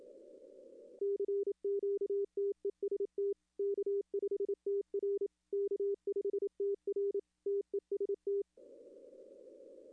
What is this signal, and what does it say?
Morse code, heard as a keyed tone, about 28 wpm, 400 Hz. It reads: CQTEST K5TR K5TR TEST
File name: contests / arrl-160-k5tr-op-w5tn -2300